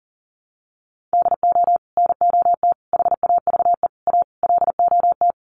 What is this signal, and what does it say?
Morse code, heard as a keyed tone, about 40 wpm, 700 Hz. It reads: DO NOT HAVE A LOT
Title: Sets of 5 Words Encore 40wpm